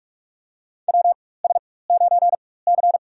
Morse code, heard as a keyed tone, about 46 wpm, spaced as for 21 wpm, 700 Hz. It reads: WS9C
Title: USA Call Signs - 4 Characters - 4x Character Spacing - 45wpm